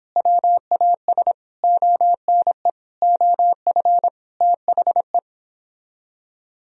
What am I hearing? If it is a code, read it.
Morse code, heard as a keyed tone, about 29 wpm, 700 Hz. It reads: WAS ONE OF THE